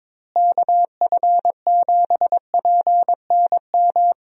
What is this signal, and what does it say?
Morse code, heard as a keyed tone, about 22 wpm, 700 Hz. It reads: KF7PNM
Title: USA Call Signs - 6 Characters - Repeated 2x - 22wpm